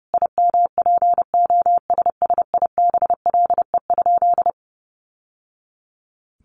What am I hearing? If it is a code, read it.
Morse code, heard as a keyed tone, about 28 wpm, 700 Hz. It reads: IMPOSSIBLE?
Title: Sentences from Top 1100 Words 30wpm